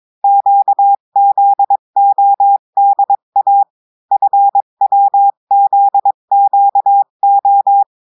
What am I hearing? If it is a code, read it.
Morse code, heard as a keyed tone, about 22 wpm, 800 Hz. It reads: QZODA FWZQO
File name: cours-du-mercredi / audios / cours-853-du-2-mai-2024-etxte chiffre et texte clair-a-20-wpm